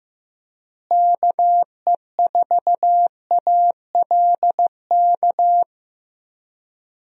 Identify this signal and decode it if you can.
Morse code, heard as a keyed tone, about 15 wpm, 700 Hz. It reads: KE4ALK
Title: USA Call Signs - 6 Characters - Repeated 3x - 15wpm